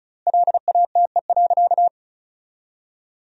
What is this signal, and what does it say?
Morse code, heard as a keyed tone, about 34 wpm, 700 Hz. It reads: LATE.
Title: 3x Word-Spacing - Sentences from Top 300 Words - 35wpm